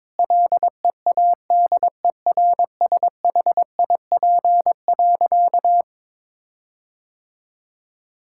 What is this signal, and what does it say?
Morse code, heard as a keyed tone, about 21 wpm, 700 Hz. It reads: LEADERSHIP.
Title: Sentences from Top 1600 Words 22wpm